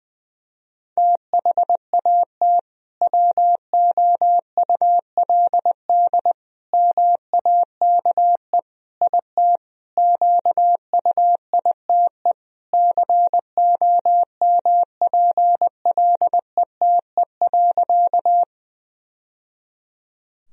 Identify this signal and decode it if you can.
Morse code, heard as a keyed tone, about 20 wpm, 700 Hz. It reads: THAT WOULD MAKE IT QUITE COMPLETE.
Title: Sentences from Top 1000 Words - Review Part 2 - 20wpm